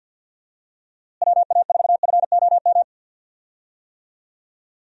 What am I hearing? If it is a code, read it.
Morse code, heard as a keyed tone, about 49 wpm, 700 Hz. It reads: WA4FOM
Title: USA Call Signs - 6 Characters - Repeated 3x - 50wpm